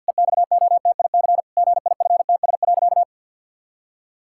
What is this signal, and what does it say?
Morse code, heard as a keyed tone, about 50 wpm, 700 Hz. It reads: EXOTIC GIFTS.
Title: Pangram Challenge v2 - 50wpm